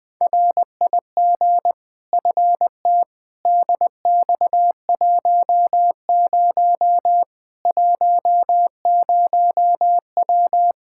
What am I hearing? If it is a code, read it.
Morse code, heard as a keyed tone, about 20 wpm, 700 Hz. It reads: RIG FT DX10 10W
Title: Practice QSO - Exchange 2 - Info - 20wpm